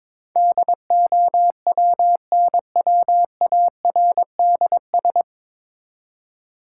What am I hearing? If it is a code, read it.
Morse code, heard as a keyed tone, about 22 wpm, 700 Hz. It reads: DOWNWARDS
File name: 9 Letter Words - First 3 Letters Given - 22wpm